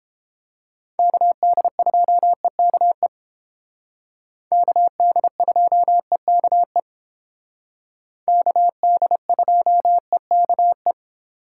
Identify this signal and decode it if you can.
Morse code, heard as a keyed tone, about 30 wpm, 700 Hz. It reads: KD2EKE KD2EKE KD2EKE